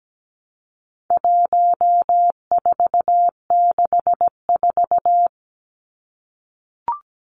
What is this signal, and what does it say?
Morse code, heard as a keyed tone, about 17 wpm, 700 Hz. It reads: 1464
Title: Numbers - 4-Digit 17wpm